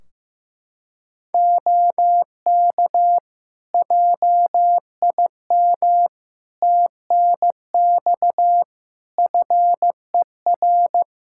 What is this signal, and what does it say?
Morse code, heard as a keyed tone, about 15 wpm, 700 Hz. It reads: OK JIM TNX FER
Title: Practice QSO - Exchange 4 - Ending - 15wpm